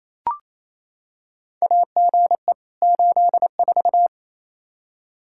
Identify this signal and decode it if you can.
Morse code, heard as a keyed tone, about 28 wpm, 700 Hz. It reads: AGE 84
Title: Practice QSO - Exchange 3 - More Info - 28wpm